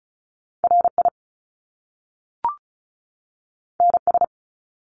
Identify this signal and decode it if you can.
Morse code, heard as a keyed tone, about 36 wpm, 700 Hz. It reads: RI NS
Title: Top Two Letter Combinations in Words 35wpm